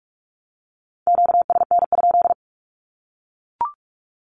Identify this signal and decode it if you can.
Morse code, heard as a keyed tone, about 44 wpm, 700 Hz. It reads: QSN?
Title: Q-Codes Repeated 3x 45wpm